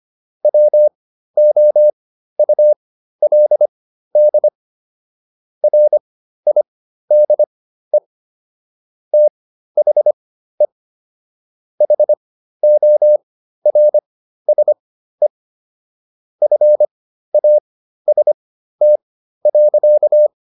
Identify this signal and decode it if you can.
Morse code, heard as a keyed tone, about 25 wpm, spaced as for 13 wpm, 600 Hz. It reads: WOULD RIDE THE HORSE FAST.